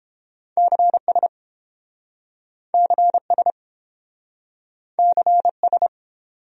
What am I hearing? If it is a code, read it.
Morse code, heard as a keyed tone, about 30 wpm, 700 Hz. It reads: CS CS CS